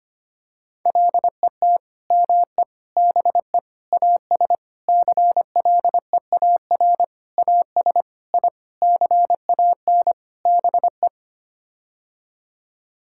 Morse code, heard as a keyed tone, about 25 wpm, 700 Hz. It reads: LET ME BE AS CLEAR AS I CAN BE
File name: Sets of 9 Words - 25wpm